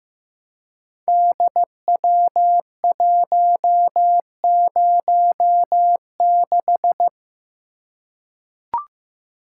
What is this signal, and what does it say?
Morse code, heard as a keyed tone, about 15 wpm, 700 Hz. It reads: DW106
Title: Code Groups - 5 Letter:Numbers - 15wpm